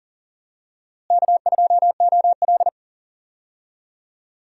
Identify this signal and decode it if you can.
Morse code, heard as a keyed tone, about 40 wpm, 700 Hz. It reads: K2OL
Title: USA Call Signs - 4 Characters - Repeated 3x - 40wpm